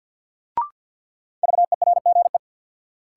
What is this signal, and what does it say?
Morse code, heard as a keyed tone, about 47 wpm, 700 Hz. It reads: VERGE
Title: Sets of 4 Words - New Word Drill - 50wpm